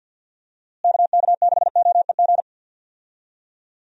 Morse code, heard as a keyed tone, about 52 wpm, 700 Hz. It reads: KK6OEG
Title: USA Call Signs - 6 Characters - Repeated 3x - 50wpm